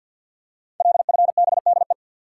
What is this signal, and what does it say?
Morse code, heard as a keyed tone, about 50 wpm, 700 Hz. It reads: LFBDE
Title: Code Groups - 5 Letters - 50wpm